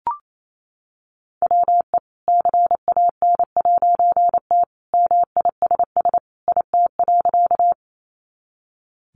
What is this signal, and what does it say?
Morse code, heard as a keyed tone, about 28 wpm, 700 Hz. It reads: WE CAN'T MISS IT.
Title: A Sound Of Thunder 28wpm